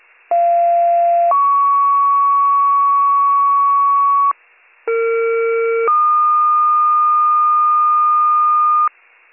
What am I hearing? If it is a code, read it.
Two-tone page, heard as a one-second tone, then a three-second tone, about 690 then 1120 Hz, about 460 then 1220 Hz. It is heard from 0.3 to 8.9 s.